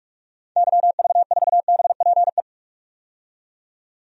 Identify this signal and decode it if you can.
Morse code, heard as a keyed tone, about 46 wpm, 700 Hz. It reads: YV4BPE
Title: DXCC-100 Call Signs - 6 Characters - 45wpm